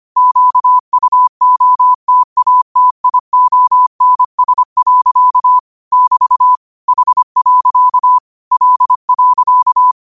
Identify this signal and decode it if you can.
Morse code, heard as a keyed tone, about 25 wpm, 1000 Hz. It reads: QUOTATIONS. <BT> H. L.
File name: qotdfiles / 25wpm